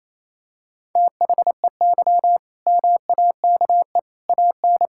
Morse code, heard as a keyed tone, about 28 wpm, 700 Hz. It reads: THEY MAKE AN
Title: Sentences from Top 1000 Words - Review Part 6 - 28wpm